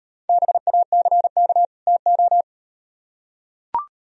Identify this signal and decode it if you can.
Morse code, heard as a keyed tone, about 38 wpm, 700 Hz. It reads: BACK TO